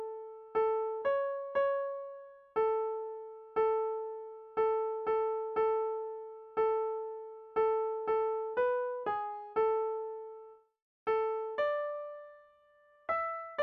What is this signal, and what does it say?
There is no code.